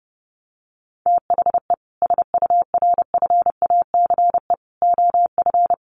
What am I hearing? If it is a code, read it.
Morse code, heard as a keyed tone, about 28 wpm, 700 Hz. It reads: THE SURFACE OF
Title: Sets of 4 Words 30wpm